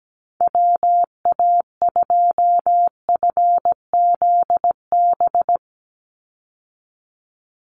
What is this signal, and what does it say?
Morse code, heard as a keyed tone, about 17 wpm, 700 Hz. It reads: WA2FZB
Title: Call Signs Repeated 2x - 17wpm